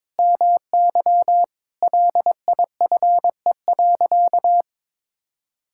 Morse code, heard as a keyed tone, about 22 wpm, 700 Hz. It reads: MY LIFE.
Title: Sentences from Top 1000 Words - Review Part 5 - 22wpm